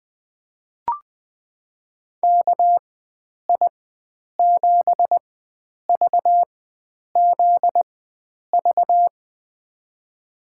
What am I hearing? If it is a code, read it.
Morse code, heard as a keyed tone, about 21 wpm, spaced as for 9 wpm, 700 Hz. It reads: KI7VZV
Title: USA Call Signs - 6 Characters - 4x Character Spacing - 20wpm